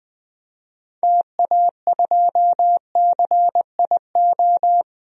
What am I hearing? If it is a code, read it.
Morse code, heard as a keyed tone, about 20 wpm, 700 Hz. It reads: TA2CIO
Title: DXCC-100 Call Signs - 6 Characters - 20wpm